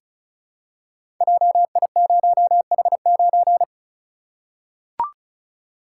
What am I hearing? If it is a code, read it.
Morse code, heard as a keyed tone, about 36 wpm, 700 Hz. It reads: JI0H9
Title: Code Groups - 5 Letter:Numbers - 35wpm